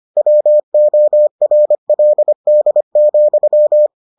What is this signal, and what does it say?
Morse code, heard as a keyed tone, about 25 wpm, 600 Hz. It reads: WORLD,